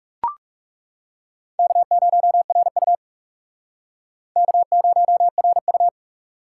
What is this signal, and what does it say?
Morse code, heard as a keyed tone, about 37 wpm, 700 Hz. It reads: K0RU K0RU
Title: Speed Racing - USA 1x2 Call Signs 30wpm